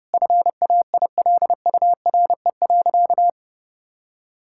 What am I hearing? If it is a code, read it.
Morse code, heard as a keyed tone, about 30 wpm, 700 Hz. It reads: FAILURE.
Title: Sentences from Top 1000 Words - Review Part 6 - 30wpm